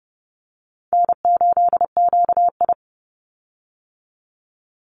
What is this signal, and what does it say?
Morse code, heard as a keyed tone, about 30 wpm, 700 Hz. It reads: N8QI
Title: USA Call Signs 30wpm